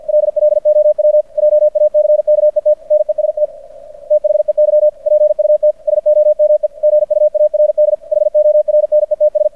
Morse code, heard as a keyed tone, about 50 wpm, 600 Hz. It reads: LLOW JACKET NEST THEY PUT SOME GUARD HORNETS
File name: archived / CRSnet